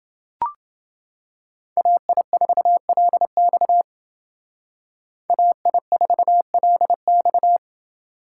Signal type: Morse code

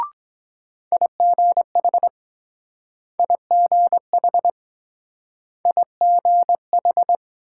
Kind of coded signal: Morse code

